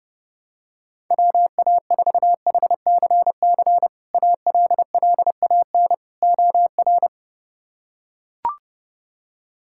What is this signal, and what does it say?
Morse code, heard as a keyed tone, about 32 wpm, 700 Hz. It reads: WA4HCC ALLAN OR
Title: K1USN SST QSO Practice - 30wpm